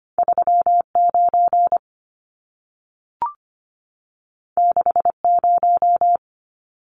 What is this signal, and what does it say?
Morse code, heard as a keyed tone, about 25 wpm, 700 Hz. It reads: 39 60
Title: Numbers - 2-Digit 25wpm